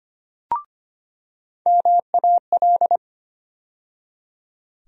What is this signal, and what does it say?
Morse code, heard as a keyed tone, about 25 wpm, 700 Hz. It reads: MAL
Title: USA Call Sign Postfixes - 3 Letters 25wpm